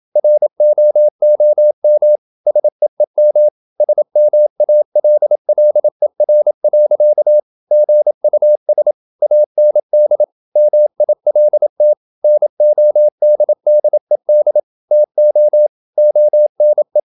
Morse code, heard as a keyed tone, about 27 wpm, 600 Hz. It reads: ROOM SEEM SMALLER. GUS AND MILT NODDED TO ONE
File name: Never 0027wpm